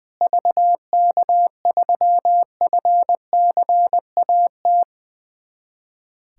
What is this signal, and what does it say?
Morse code, heard as a keyed tone, about 20 wpm, 700 Hz. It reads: VK3FCAT
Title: USA Callsign DE Callsign 20wpm